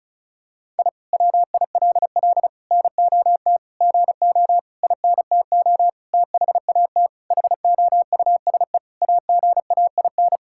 Morse code, heard as a keyed tone, about 35 wpm, 700 Hz. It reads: I WILL NOT GO INTO THAT HOUSE AGAIN
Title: Sentences from Top 1000 Words - Review Part 6 - 35wpm